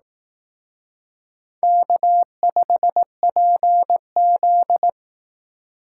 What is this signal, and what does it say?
Morse code, heard as a keyed tone, about 18 wpm, 700 Hz. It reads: K5PZ
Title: Speed Racing - USA 1x2 Call Signs 15wpm